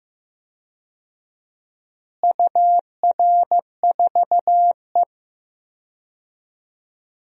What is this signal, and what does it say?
Morse code, heard as a keyed tone, about 15 wpm, 700 Hz. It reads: UR4E
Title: DXCC-100 Call Signs - 4 Characters - Repeated 2x - 15wpm